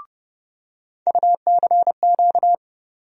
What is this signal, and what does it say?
Morse code, heard as a keyed tone, about 30 wpm, 700 Hz. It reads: UCQ
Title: USA Call Sign Postfixes - 3 Letters 30wpm